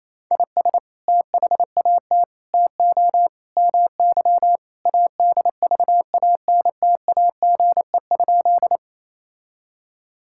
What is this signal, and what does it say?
Morse code, heard as a keyed tone, about 28 wpm, 700 Hz. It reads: IS THAT TO MY ADVANTAGE?
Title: Sentences from Top 1000 Words - Review Part 6 - 28wpm